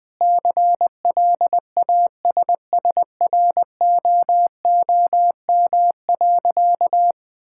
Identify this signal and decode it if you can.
Morse code, heard as a keyed tone, about 20 wpm, 700 Hz. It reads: CLASSROOM.
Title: Difficult Sentences with Hints - 20wpm